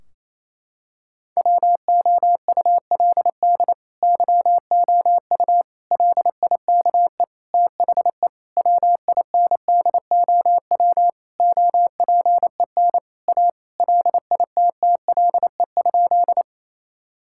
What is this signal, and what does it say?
Morse code, heard as a keyed tone, about 28 wpm, 700 Hz. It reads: WOULD YOU LIKE THE WINDOW OPEN A LITTLE?
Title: Sentences from Top 1600 Words 28wpm